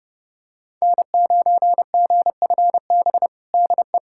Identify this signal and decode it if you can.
Morse code, heard as a keyed tone, about 32 wpm, 700 Hz. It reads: N9GFB DE
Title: USA Callsign DE Callsign - 5 Characters - 30wpm